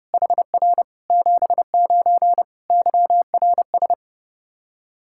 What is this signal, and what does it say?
Morse code, heard as a keyed tone, about 30 wpm, 700 Hz. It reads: HR 79 YRS
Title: Practice QSO - Exchange 3 - More Info - 30wpm